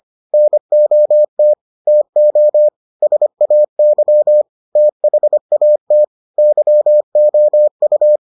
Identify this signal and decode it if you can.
Morse code, heard as a keyed tone, about 25 wpm, 600 Hz. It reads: NOT TO SAY THAT YOU